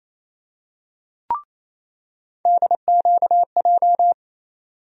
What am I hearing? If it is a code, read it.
Morse code, heard as a keyed tone, about 28 wpm, 700 Hz. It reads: DQJ
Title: USA Call Sign Postfixes - 3 Letters 28wpm